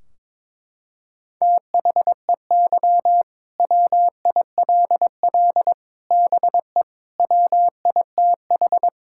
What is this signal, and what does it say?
Morse code, heard as a keyed tone, about 22 wpm, 700 Hz. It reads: THEY WILL BE WITH
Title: Sentences from Top 1600 Words 22wpm